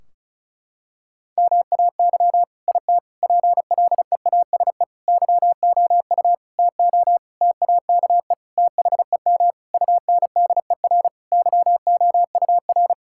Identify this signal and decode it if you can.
Morse code, heard as a keyed tone, about 34 wpm, 700 Hz. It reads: MAY IT PLEASE YOU TO TAKE THEM UNDER YOUR
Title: Sentences from Top 1600 Words 35wpm